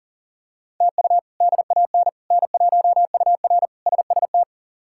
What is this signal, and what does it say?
Morse code, heard as a keyed tone, about 40 wpm, 700 Hz. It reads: TU DAN N1UR SST
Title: K1USN SST QSO Practice - 40wpm